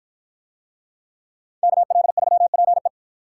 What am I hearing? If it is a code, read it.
Morse code, heard as a keyed tone, about 52 wpm, 700 Hz. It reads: KL3PE